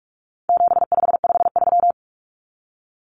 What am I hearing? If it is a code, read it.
Morse code, heard as a keyed tone, about 44 wpm, 700 Hz. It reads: 7553
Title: Numbers - 4-Digit 45wpm